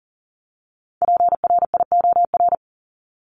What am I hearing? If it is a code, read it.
Morse code, heard as a keyed tone, about 40 wpm, 700 Hz. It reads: PRIOR
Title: Top 1501-1750 Words 40wpm